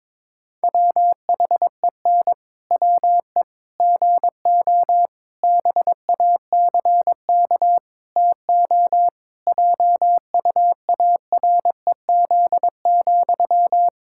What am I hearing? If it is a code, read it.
Morse code, heard as a keyed tone, about 22 wpm, 700 Hz. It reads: WHEN WE GO BACK TO JUAREZ,
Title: Pangram Challenge v2 - 22wpm